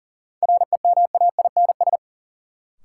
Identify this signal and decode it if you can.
Morse code, heard as a keyed tone, about 40 wpm, 700 Hz. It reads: REMAINS
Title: Sets of 5 Words Encore - New Word Drill - 40wpm